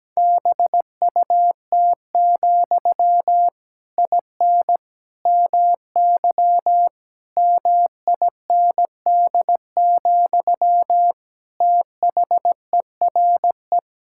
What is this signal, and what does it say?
Morse code, heard as a keyed tone, about 17 wpm, 700 Hz. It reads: BUT, IN MY MIND, THERE
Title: Sentences from Top 1000 Words - Review Part 5 - 17wpm